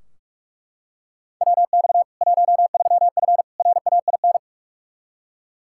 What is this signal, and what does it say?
Morse code, heard as a keyed tone, about 43 wpm, 700 Hz. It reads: WX 13F RAIN
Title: K1USN SST QSO Practice 45wpm